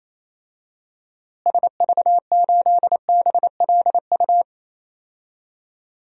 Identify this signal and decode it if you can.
Morse code, heard as a keyed tone, about 27 wpm, 700 Hz. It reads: SV8BLU
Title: DXCC-100 Call Signs - 6 Characters - Repeated 2x - 28wpm